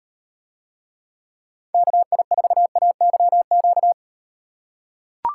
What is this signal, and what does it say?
Morse code, heard as a keyed tone, about 38 wpm, 700 Hz. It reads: KI4AYQ